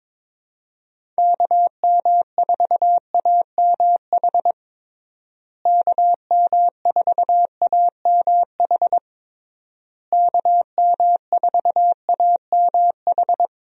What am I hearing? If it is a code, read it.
Morse code, heard as a keyed tone, about 22 wpm, 700 Hz. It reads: KM4AMH KM4AMH KM4AMH
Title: USA Call Signs - 6 Characters - Repeated 3x - 22wpm